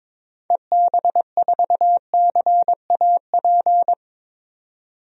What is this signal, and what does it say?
Morse code, heard as a keyed tone, about 21 wpm, 700 Hz. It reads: EB4CAP
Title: DXCC-100 Call Signs - 6 Characters - 22wpm